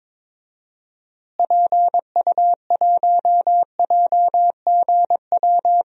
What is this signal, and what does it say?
Morse code, heard as a keyed tone, about 22 wpm, 700 Hz. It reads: PU1JGW
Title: DXCC-100 Call Signs - 6 Characters - Repeated 2x - 22wpm